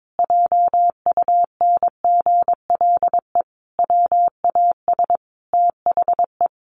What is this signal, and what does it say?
Morse code, heard as a keyed tone, about 22 wpm, 700 Hz. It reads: JUNGLE WAS THE
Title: A Sound Of Thunder 22wpm